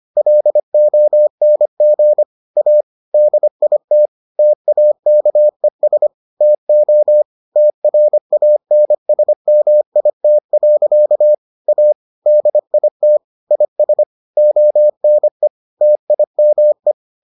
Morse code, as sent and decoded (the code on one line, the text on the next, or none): .-.. --- -. --. / .- / -.. .. - / - .- -.- . ... / - --- / - .-. .- -. ... -- .. - .-.-.- / .- / -.. .. - / .. ... / --- -. / - .. -- .
LONG A DIT TAKES TO TRANSMIT. A DIT IS ONE TIME